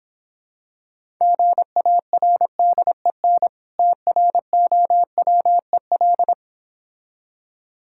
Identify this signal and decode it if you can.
Morse code, heard as a keyed tone, about 26 wpm, 700 Hz. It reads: GARDEN TROWEL